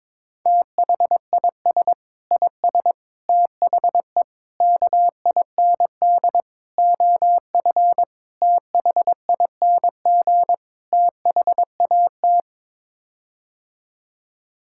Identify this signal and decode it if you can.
Morse code, heard as a keyed tone, about 22 wpm, 700 Hz. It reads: THIS IS THE KIND OF THING THAT
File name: practice / 22wpm / Sets of 7 Words - 22wpm